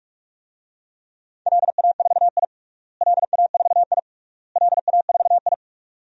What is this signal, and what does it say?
Morse code, heard as a keyed tone, about 46 wpm, 700 Hz. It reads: LA4I LA4I LA4I
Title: DXCC-100 Call Signs - 4 Characters - Repeated 3x - 45wpm